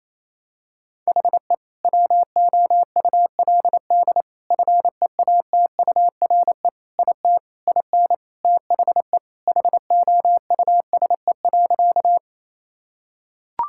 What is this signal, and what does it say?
Morse code, heard as a keyed tone, about 28 wpm, 700 Hz. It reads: HE WOULD FEATURE IT IN THE HOUSE.
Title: Sentences from Top 1000 Words - Review Part 5 - 28wpm